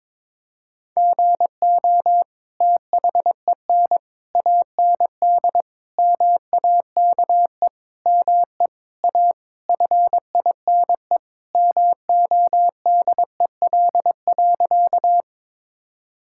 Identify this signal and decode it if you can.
Morse code, heard as a keyed tone, about 22 wpm, 700 Hz. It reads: GO THEN AND MAKE ME A FINE MODEL.
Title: Sentences from Top 1000 Words - Review Part 5 - 22wpm